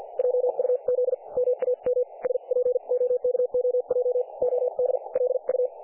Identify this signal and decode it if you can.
Morse code, heard as a keyed tone, about 49 wpm, spaced as for 41 wpm, 500 Hz. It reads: JUZ MAM SPOKOJ OD LU
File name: dlazaawansowanych / nagraniazpasma / websdr recording 3564.7kHz